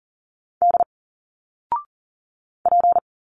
Morse code, heard as a keyed tone, about 40 wpm, 700 Hz. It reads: D P
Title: Single Letters 40wpm